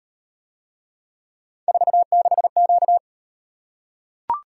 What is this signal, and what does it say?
Morse code, heard as a keyed tone, about 37 wpm, 700 Hz. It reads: 46Q